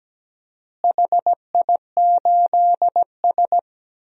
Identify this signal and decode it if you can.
Morse code, heard as a keyed tone, about 17 wpm, 700 Hz. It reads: HI8S
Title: DXCC-100 Call Signs - 4 Characters - Repeated 3x - 17wpm